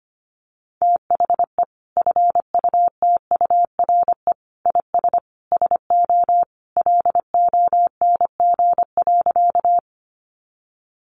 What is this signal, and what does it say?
Morse code, heard as a keyed tone, about 25 wpm, 700 Hz. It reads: THE FUTURE IS SO LONG.
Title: Sentences from Top 700 Words 25wpm